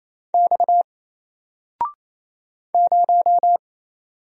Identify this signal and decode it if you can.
Morse code, heard as a keyed tone, about 27 wpm, 700 Hz. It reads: X 0
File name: Single-Letter-Number-28wpm